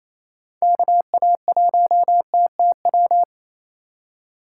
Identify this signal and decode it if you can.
Morse code, heard as a keyed tone, about 27 wpm, 700 Hz. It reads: KA1TTW
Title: USA Call Signs - 6 Characters - 8x Character Spacing - 28wpm